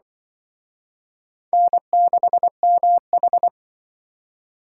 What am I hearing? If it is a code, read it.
Morse code, heard as a keyed tone, about 25 wpm, 700 Hz. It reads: N6MH
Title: Speed Racing - USA 1x2 Call Signs 20wpm